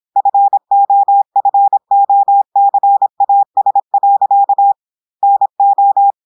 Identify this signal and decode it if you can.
Morse code, heard as a keyed tone, about 26 wpm, 800 Hz. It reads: FOFOCAS. NO
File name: portugais texte en clair 2 vitesse 26 mots minutes